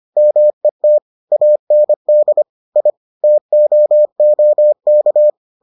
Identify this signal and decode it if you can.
Morse code, heard as a keyed tone, about 25 wpm, 600 Hz. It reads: MET AND I TOOK